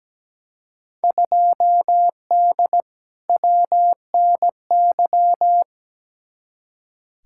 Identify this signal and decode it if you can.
Morse code, heard as a keyed tone, about 17 wpm, 700 Hz. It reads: 2D WNY
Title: Field Day Exchanges - 17wpm